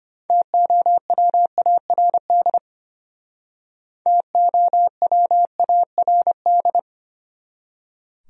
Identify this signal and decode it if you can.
Morse code, heard as a keyed tone, about 33 wpm, 700 Hz. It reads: TOWARD TOWARD